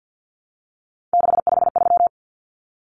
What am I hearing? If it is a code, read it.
Morse code, heard as a keyed tone, about 50 wpm, 700 Hz. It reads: 653
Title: Numbers - 3-Digit 50wpm